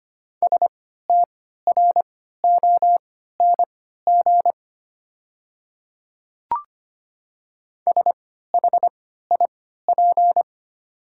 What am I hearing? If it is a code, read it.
Morse code, heard as a keyed tone, about 25 wpm, spaced as for 14 wpm, 700 Hz. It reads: STRONG SHIP
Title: 3x Character Spacing - Top 500 Words - 25wpm